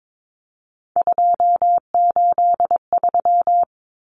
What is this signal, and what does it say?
Morse code, heard as a keyed tone, about 22 wpm, 700 Hz. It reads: 283
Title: Common Cut Numbers - 3-Digit 22wpm